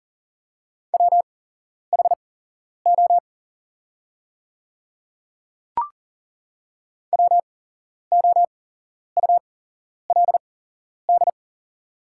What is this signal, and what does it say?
Morse code, heard as a keyed tone, about 40 wpm, spaced as for 11 wpm, 690 Hz. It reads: WHO WOULD
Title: 8x Character Spacing - Top 100 Words - 40wpm